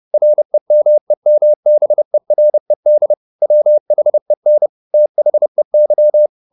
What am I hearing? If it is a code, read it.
Morse code, heard as a keyed tone, about 30 wpm, 600 Hz. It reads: REMEMBERED WHEN THEY
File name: the old man and the sea / the old man and the sea 1 30wpm